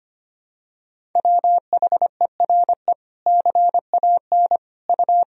Morse code, heard as a keyed tone, about 25 wpm, 700 Hz. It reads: WHERE CAN U
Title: Sentences from Top 500 Words and Abbreviations 25wpm